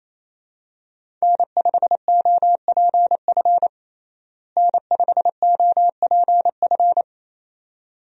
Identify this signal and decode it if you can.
Morse code, heard as a keyed tone, about 27 wpm, 700 Hz. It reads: N5OPF N5OPF
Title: USA Call Signs - 5 Characters - Repeated 2x - 28wpm